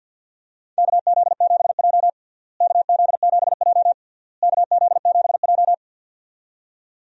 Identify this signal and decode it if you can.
Morse code, heard as a keyed tone, about 52 wpm, 700 Hz. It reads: KZ7J KZ7J KZ7J
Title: USA Call Signs - 4 Characters - Repeated 3x - 50wpm